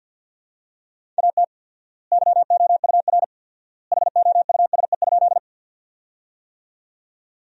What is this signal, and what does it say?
Morse code, heard as a keyed tone, about 47 wpm, 700 Hz. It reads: AT YOUR HOUSE?
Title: 4x Word-Spacing - Sentences from Top 200 Words - 50wpm